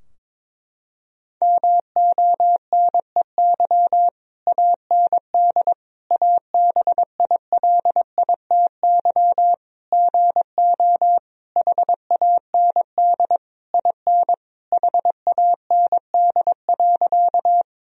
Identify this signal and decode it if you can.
Morse code, heard as a keyed tone, about 22 wpm, 700 Hz. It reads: MONEY AND ABILITY GO HAND IN HAND.
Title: Sentences from Top 1000 Words - Review Part 3 - 22wpm